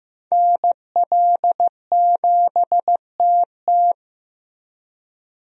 Morse code, heard as a keyed tone, about 15 wpm, 700 Hz. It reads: NL7TT
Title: USA Call Signs - 5 Characters - Repeated 2x - 15wpm